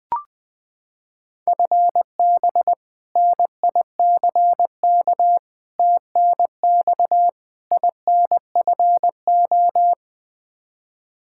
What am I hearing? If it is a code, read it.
Morse code, heard as a keyed tone, about 20 wpm, 700 Hz. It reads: FB NICK TNX INFO
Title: Practice QSO - Exchange 3 - More Info - 20wpm